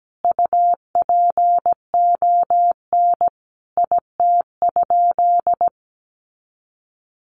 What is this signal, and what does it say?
Morse code, heard as a keyed tone, about 17 wpm, 700 Hz. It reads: UPON IT?
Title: Sentences from Top 1500 Words 17wpm